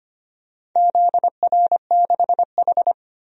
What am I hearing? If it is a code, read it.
Morse code, heard as a keyed tone, about 25 wpm, 700 Hz. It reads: ZR6H
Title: DXCC-100 Call Signs - 4 Characters - 25wpm